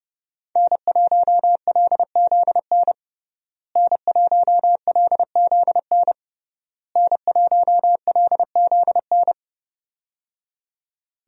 Morse code, heard as a keyed tone, about 29 wpm, 700 Hz. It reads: N1LZN N1LZN N1LZN
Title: USA Call Signs - 5 Characters - Repeated 3x - 30wpm